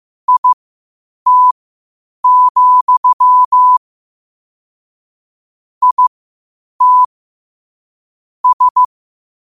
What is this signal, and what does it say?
Morse code, heard as a keyed tone, about 15 wpm, spaced as for 8 wpm, 1000 Hz. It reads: IT, IT S
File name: qotdfiles / 8wpm